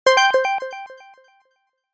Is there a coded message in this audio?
No.